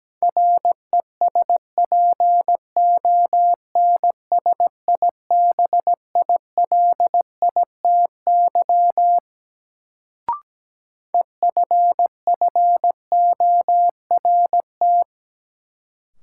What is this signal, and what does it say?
Morse code, heard as a keyed tone, about 17 wpm, 700 Hz. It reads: RESPONSIBILITY EFFORT
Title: Sentences from Top 1200 Words - New Word Drill - 17wpm